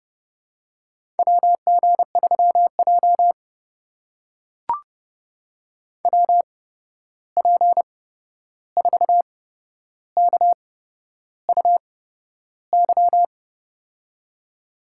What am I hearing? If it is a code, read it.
Morse code, heard as a keyed tone, about 30 wpm, spaced as for 11 wpm, 700 Hz. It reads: WG3J WP4KUY
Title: USA Call Signs - 8x Character Spacing - 30wpm